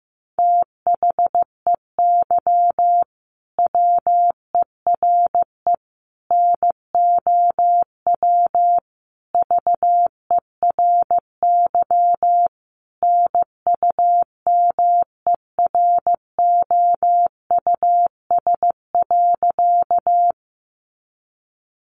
Morse code, heard as a keyed tone, about 15 wpm, 700 Hz. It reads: THEY WERE NOW VERY NUMEROUS.
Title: Sentences from Top 1400 Words 15wpm